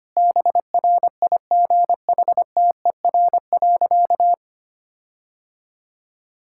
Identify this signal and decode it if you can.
Morse code, heard as a keyed tone, about 25 wpm, 700 Hz. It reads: BRIGHTER.
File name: practice / 25wpm / All-Age Appropriate Jokes - 25wpm